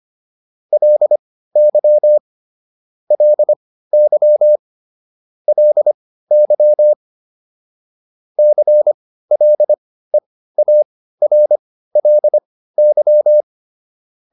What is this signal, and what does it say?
Morse code, heard as a keyed tone, about 25 wpm, spaced as for 15 wpm, 600 Hz. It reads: LY LY LY CLEARLY